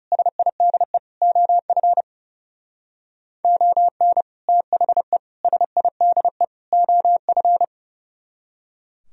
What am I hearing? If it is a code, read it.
Morse code, heard as a keyed tone, about 35 wpm, 700 Hz. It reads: SIDE OF ON THE SIDE OF
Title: Speed-Racing - Sets of 4 Words 30wpm